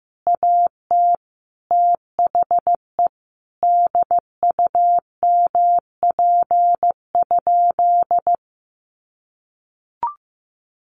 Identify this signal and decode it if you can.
Morse code, heard as a keyed tone, about 15 wpm, 700 Hz. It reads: AT THE DUMP?
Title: Sentences from Top 1300 Words 15wpm